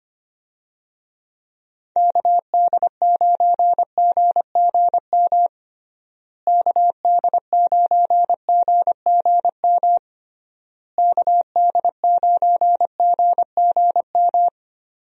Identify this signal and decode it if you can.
Morse code, heard as a keyed tone, about 25 wpm, 700 Hz. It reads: KD9GGM KD9GGM KD9GGM
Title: USA Call Signs - 6 Characters - Repeated 3x - 25wpm